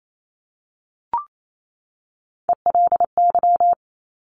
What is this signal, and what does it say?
Morse code, heard as a keyed tone, about 28 wpm, 700 Hz. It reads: ELY
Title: Top Three Letter Combinations in Words 28wpm